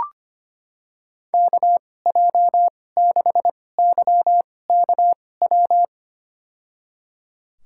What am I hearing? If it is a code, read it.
Morse code, heard as a keyed tone, about 25 wpm, spaced as for 18 wpm, 700 Hz. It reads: KJ6YKW
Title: USA Call Signs - 2x Character Spacing - 25wpm